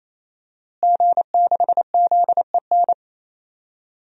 Morse code, heard as a keyed tone, about 27 wpm, 700 Hz. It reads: G6ZEN
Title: DXCC-100 Call Signs - 5 Characters - Repeated 3x - 28wpm